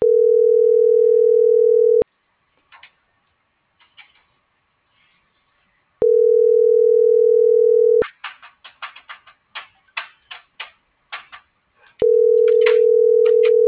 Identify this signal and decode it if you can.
Ringback tone (North America), 440 + 480 Hz.